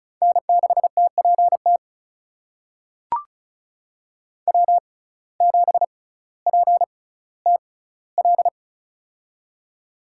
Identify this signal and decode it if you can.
Morse code, heard as a keyed tone, about 36 wpm, spaced as for 18 wpm, 700 Hz. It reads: N6TPT W7PTL
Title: USA Call Signs - 6x Character Spacing - 35wpm